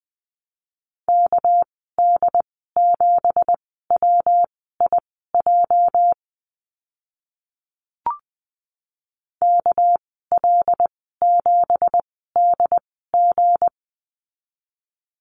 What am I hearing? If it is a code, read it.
Morse code, heard as a keyed tone, about 20 wpm, spaced as for 14 wpm, 700 Hz. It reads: KD7WIJ KL7DG
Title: 207 Lesson 33g - Letter j - Call Signs 20wpm